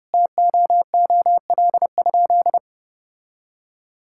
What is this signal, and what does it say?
Morse code, heard as a keyed tone, about 30 wpm, 700 Hz. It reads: TOOL?
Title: Sentences from Top 1000 Words - Review Part 6 - 30wpm